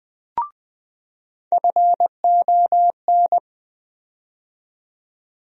Morse code, heard as a keyed tone, about 20 wpm, 700 Hz. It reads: FON
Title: USA Call Sign Postfixes - 3 Letters 20wpm